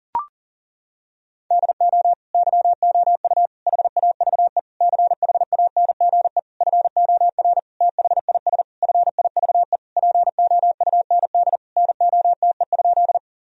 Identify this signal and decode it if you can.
Morse code, heard as a keyed tone, about 40 wpm, 700 Hz. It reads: DO YOU HAVE CHANGE FOR THIS FIVE POUND NOTE?
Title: Sentences from Top 1000 Words - Review Part 2 - 40wpm